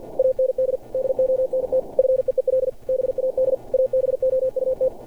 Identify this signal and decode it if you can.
Morse code, heard as a keyed tone, about 53 wpm, 520 Hz. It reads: AND DONT FEEL BAD ABOUT
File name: archived / CRSnet